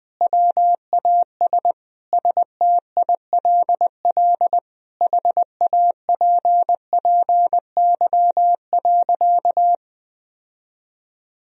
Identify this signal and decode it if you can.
Morse code, heard as a keyed tone, about 20 wpm, 700 Hz. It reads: WAS STILL HAPPY.
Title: Sentences from Top 1000 Words - Review Part 3 - 20wpm